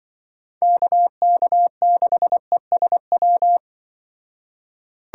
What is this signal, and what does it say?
Morse code, heard as a keyed tone, about 24 wpm, 700 Hz. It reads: KK6ESW
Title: Speed Racing - USA 2x3 Call Signs 20wpm